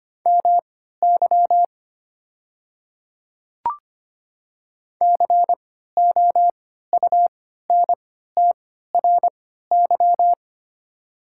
Morse code, heard as a keyed tone, about 25 wpm, spaced as for 14 wpm, 700 Hz. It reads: MY COUNTRY
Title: 3x Character Spacing - Top 500 Words - 25wpm